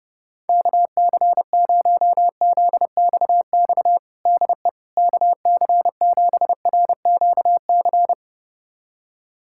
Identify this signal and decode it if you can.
Morse code, heard as a keyed tone, about 30 wpm, 700 Hz. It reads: KC0ZXX DE KC7RQC
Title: USA Callsign DE Callsign 30wpm